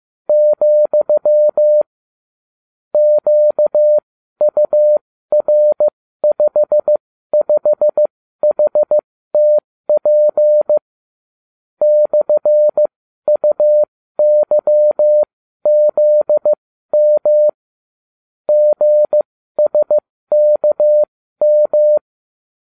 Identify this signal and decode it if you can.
Morse code, heard as a keyed tone, about 15 wpm, spaced as for 12 wpm, 600 Hz. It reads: , QUR55HTP /UYZM GSKM